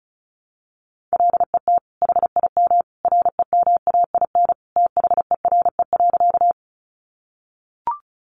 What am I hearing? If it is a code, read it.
Morse code, heard as a keyed tone, about 34 wpm, 700 Hz. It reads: LET HIM REMAIN THERE.
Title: Sentences from Top 1200 Words 35wpm